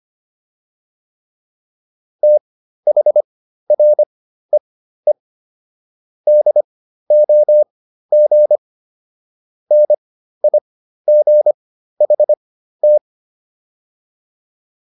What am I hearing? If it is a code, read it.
Morse code, heard as a keyed tone, about 25 wpm, spaced as for 13 wpm, 600 Hz. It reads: THREE DOG NIGHT